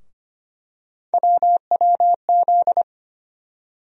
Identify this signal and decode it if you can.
Morse code, heard as a keyed tone, about 25 wpm, 700 Hz. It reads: WWZ
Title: USA Call Sign Postfixes - 3 Letters 25wpm